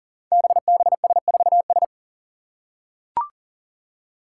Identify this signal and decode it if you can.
Morse code, heard as a keyed tone, about 40 wpm, 700 Hz. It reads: BBS4S
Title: Five Challenging Letter-Number Combinations 40wpm